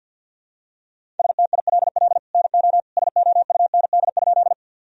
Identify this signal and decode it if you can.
Morse code, heard as a keyed tone, about 52 wpm, 700 Hz. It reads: STILL NO SOUND?
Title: Sentences from Top 1000 Words - Review Part 1 - 50wpm